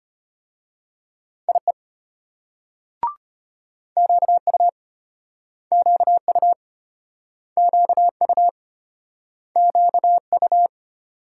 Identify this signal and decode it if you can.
Morse code, heard as a keyed tone, about 34 wpm, 700 Hz. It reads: IE QU QU QU QU